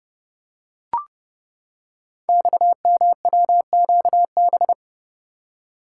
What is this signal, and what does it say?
Morse code, heard as a keyed tone, about 29 wpm, 700 Hz. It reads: XMWQB
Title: Code Groups - 5 Letter:Numbers - 30wpm